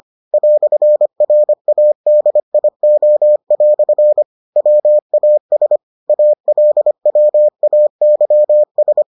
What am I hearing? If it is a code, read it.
Morse code, heard as a keyed tone, about 25 wpm, 600 Hz. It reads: "RADIO" WAS ALWAYS